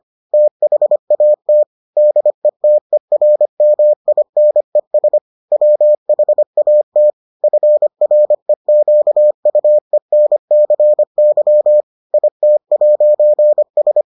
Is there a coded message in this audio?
Yes: Morse code from 0.3 to 14.0 s, heard as a keyed tone, about 25 wpm, 600 Hz. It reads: THAT DETERMINES WHAT FREQUENCY IT'S